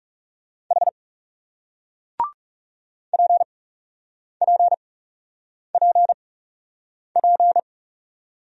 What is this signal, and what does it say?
Morse code, heard as a keyed tone, about 41 wpm, spaced as for 8 wpm, 700 Hz. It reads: H PPPP